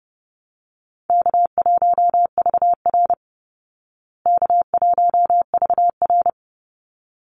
Morse code, heard as a keyed tone, about 30 wpm, 700 Hz. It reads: K1VR K1VR
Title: Easy Call Signs Repeated 2x - 30wpm